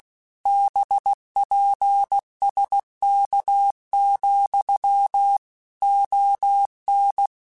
Morse code, heard as a keyed tone, about 16 wpm, 800 Hz. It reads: BPSK, ON
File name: QSOs / QSOs-mp3s / WA7QOX 2@15